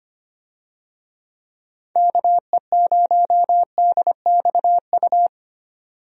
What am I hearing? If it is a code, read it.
Morse code, heard as a keyed tone, about 25 wpm, 700 Hz. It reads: KE0DXU
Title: USA Call Signs - 6 Characters - 25wpm